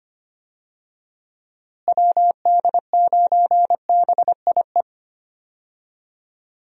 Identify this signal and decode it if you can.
Morse code, heard as a keyed tone, about 25 wpm, 700 Hz. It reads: WD9BIE
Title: USA Call Signs - 6 Characters - Repeated 3x - 25wpm